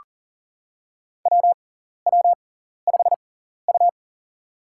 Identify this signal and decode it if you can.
Morse code, heard as a keyed tone, about 39 wpm, spaced as for 14 wpm, 700 Hz. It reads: WW5U
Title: USA Call Signs - 4 Characters - 6x Character Spacing - 40wpm